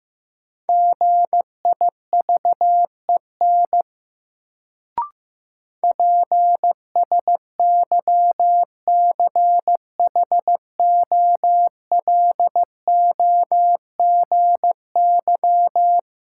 Morse code, heard as a keyed tone, about 15 wpm, 700 Hz. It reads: GIVEN PSYCHOLOGY
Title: Sentences from Top 1400 Words - New Word Drill Alt - 15wpm